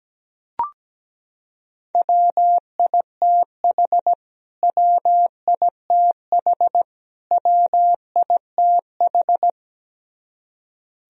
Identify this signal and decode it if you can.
Morse code, heard as a keyed tone, about 17 wpm, 700 Hz. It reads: WITH WITH WITH
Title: Top 100 Words Repeated 17wpm